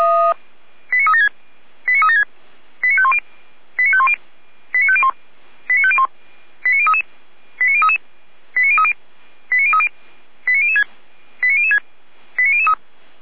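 Five-tone selective call (ZVEI), heard as quick runs of five tones, tones of about 80 ms. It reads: 89376 89376 89510 89510 89501 89501 89033 89033 89030 89030 89006 89006 89003
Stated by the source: ZVEI five-tone selective call received by SDR, Feuerwehr Orlamünde (f85360).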